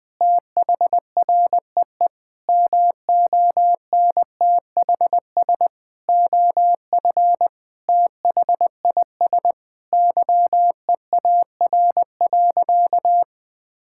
Morse code, heard as a keyed tone, about 20 wpm, 700 Hz. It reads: THREE MONTHS OF THIS YEAR.
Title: Sentences with 8-Word Phrases - Spoken Beginning - 20wpm